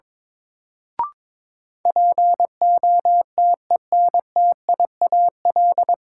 Morse code, heard as a keyed tone, about 22 wpm, 700 Hz. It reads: POTENTIAL
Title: Sentences from Top 1000 Words - Review Part 4 - New Word Drill - 22wpm